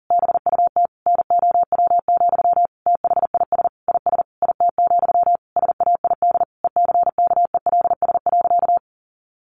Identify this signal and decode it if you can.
Morse code, heard as a keyed tone, about 40 wpm, 700 Hz. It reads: BUT NOW, THIS IS IT, SAID ECKELS.
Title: A Sound Of Thunder 40wpm